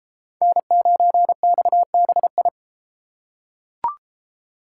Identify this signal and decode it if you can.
Morse code, heard as a keyed tone, about 33 wpm, 700 Hz. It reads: N9XBI